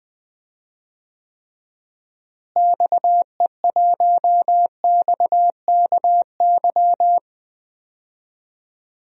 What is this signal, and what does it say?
Morse code, heard as a keyed tone, about 20 wpm, 700 Hz. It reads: XE1XKY
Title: DXCC-100 Call Signs - 6 Characters - Repeated 3x - 20wpm